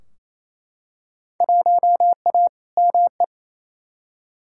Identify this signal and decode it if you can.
Morse code, heard as a keyed tone, about 28 wpm, 700 Hz. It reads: 1A ME
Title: Field Day Exchanges Encore - 28wpm